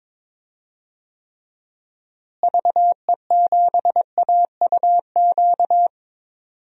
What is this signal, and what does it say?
Morse code, heard as a keyed tone, about 22 wpm, 700 Hz. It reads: VE7AUQ
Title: DXCC-100 Call Signs - 6 Characters - Repeated 3x - 22wpm